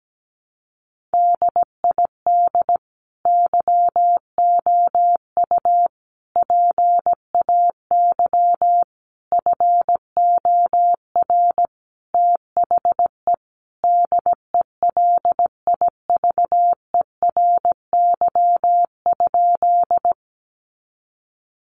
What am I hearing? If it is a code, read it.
Morse code, heard as a keyed tone, about 17 wpm, 700 Hz. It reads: DID YOU PAY FOR THE DELIVERY?
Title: Sentences from Top 1500 Words 17wpm